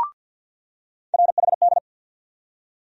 Morse code, heard as a keyed tone, about 47 wpm, 700 Hz. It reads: RHD
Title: USA Call Sign Postfixes - 3 Letters 50wpm